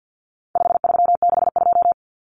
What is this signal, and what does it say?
Morse code, heard as a keyed tone, about 50 wpm, 700 Hz. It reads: 5362
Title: Common Cut Numbers - 4-Digit 50wpm